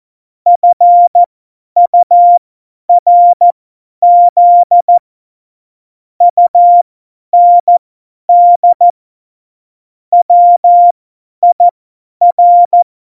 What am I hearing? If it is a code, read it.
Morse code, heard as a keyed tone, about 14 wpm, spaced as for 10 wpm, 700 Hz. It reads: FURZ UND WIR
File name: Morsekurs / Witze / Witz 25. September 2018, 50 F70